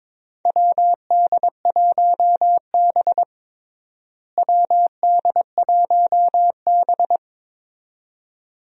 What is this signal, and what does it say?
Morse code, heard as a keyed tone, about 22 wpm, 700 Hz. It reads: WD1B WD1B
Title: USA Call Signs - 4 Characters - Repeated 2x - 22wpm